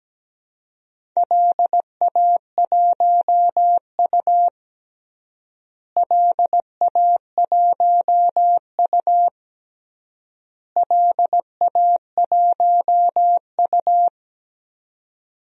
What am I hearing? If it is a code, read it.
Morse code, heard as a keyed tone, about 17 wpm, 700 Hz. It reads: LA1U LA1U LA1U
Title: DXCC-100 Call Signs - 4 Characters - Repeated 3x - 17wpm